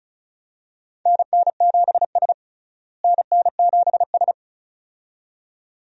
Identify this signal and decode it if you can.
Morse code, heard as a keyed tone, about 36 wpm, 700 Hz. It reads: NN7S NN7S
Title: USA Call Signs - 4 Characters - Repeated 2x - 35wpm